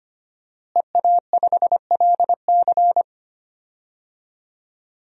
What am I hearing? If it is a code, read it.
Morse code, heard as a keyed tone, about 25 wpm, 700 Hz. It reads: EA5LC
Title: DXCC-100 Call Signs - 5 Characters - Repeated 2x - 25wpm